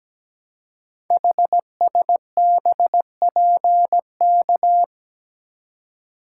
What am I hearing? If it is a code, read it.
Morse code, heard as a keyed tone, about 17 wpm, 700 Hz. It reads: HSBPK
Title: Code Groups - 5 Letters - 17wpm